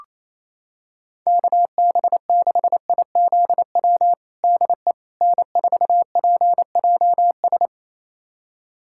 Morse code, heard as a keyed tone, about 27 wpm, 700 Hz. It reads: KB6IZW DE N4PJS
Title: USA Callsign DE Callsign 28wpm